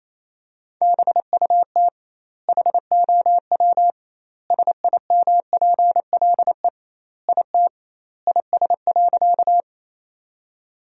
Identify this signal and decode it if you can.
Morse code, heard as a keyed tone, about 28 wpm, 700 Hz. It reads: BUT HOW SIMPLE IT IS.
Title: 2x Word-Spacing - Sentences from Top 400 Words - 28wpm